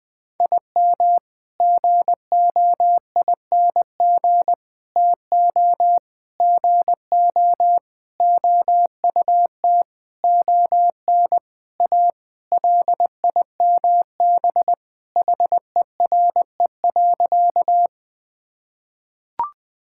Morse code, heard as a keyed tone, about 21 wpm, 700 Hz. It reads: IM GOING TO GO OUT ON A LIMB HERE.
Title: Sentences with 8-Word Phrases - 20wpm